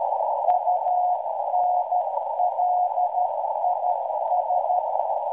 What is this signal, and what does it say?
Morse code, heard as a keyed tone, about 13 wpm, 720 Hz, with noise. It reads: UNLIS